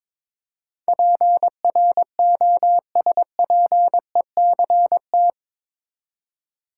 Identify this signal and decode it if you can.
Morse code, heard as a keyed tone, about 22 wpm, 700 Hz. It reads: PROSPECT
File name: practice / 22wpm / Eight Letter Words - 22wpm